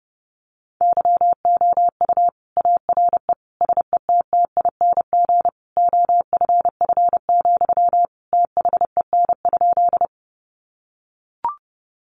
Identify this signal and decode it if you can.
Morse code, heard as a keyed tone, about 28 wpm, 700 Hz. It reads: YOU ARE SETTING OFF, THEN?
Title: Sentences from Top 1300 Words 30wpm